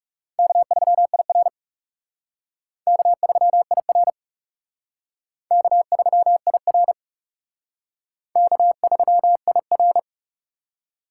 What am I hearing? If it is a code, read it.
Morse code, heard as a keyed tone, about 36 wpm, 700 Hz. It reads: K3IR K3IR K3IR K3IR
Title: Speed Racing - USA 1x2 Call Signs 30wpm